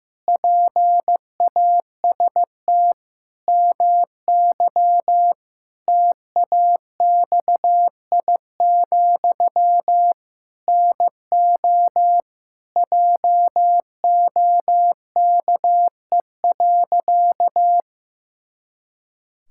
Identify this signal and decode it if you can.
Morse code, heard as a keyed tone, about 15 wpm, 700 Hz. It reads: PAST MY TAXI, NO JOKE.
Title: Pangram Challenge v2 - 15wpm